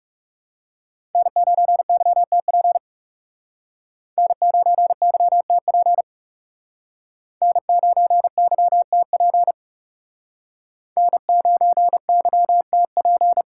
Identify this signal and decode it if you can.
Morse code, heard as a keyed tone, about 39 wpm, 700 Hz. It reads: N9YTP N9YTP N9YTP N9YTP